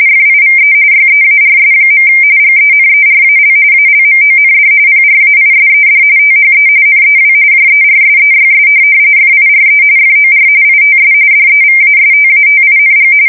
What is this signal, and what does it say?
RTTY, about 45.45 baud. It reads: FIVE YEARS. GENERAL ALMOND PRAISED THE NEW JERSEYS FIRE FOR ITS DEVASTATING
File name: fsk170 / USS-New-Jersey